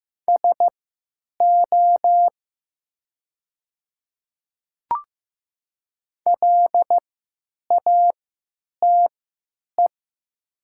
Morse code, heard as a keyed tone, about 16 wpm, spaced as for 9 wpm, 700 Hz. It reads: SO LATE
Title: 3x Character Spacing - Top 300 Words - 15wpm